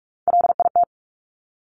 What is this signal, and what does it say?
Morse code, heard as a keyed tone, about 46 wpm, 700 Hz. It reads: LIT
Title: Top Three Letter Combinations in Words 45wpm